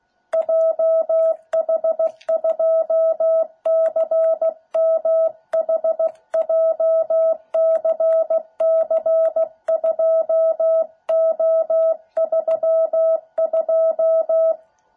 Morse code, heard as a keyed tone, about 16 wpm, 660 Hz. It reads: JH2CMH JCC2O32